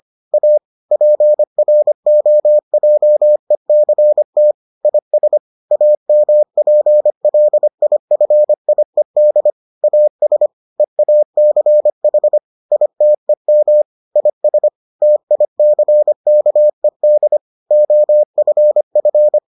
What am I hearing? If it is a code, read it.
Morse code, heard as a keyed tone, about 25 wpm, 600 Hz. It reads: A PROJECT IS AMPLIFIED AS EACH ITEM IS TICKED OFF